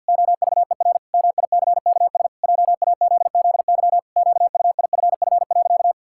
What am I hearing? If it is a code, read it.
Morse code, heard as a keyed tone, about 52 wpm, 700 Hz. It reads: OVER MICKS JAZZY QUIFF.
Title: Pangram Challenge v2 - 50wpm